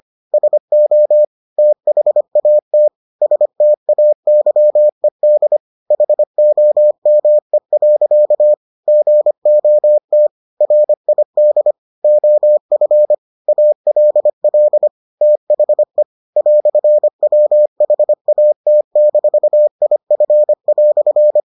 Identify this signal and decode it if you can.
Morse code, heard as a keyed tone, about 25 wpm, 600 Hz. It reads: SO THAT STAYED HOME. GOT RID OF ALL THE "WHAT-IF"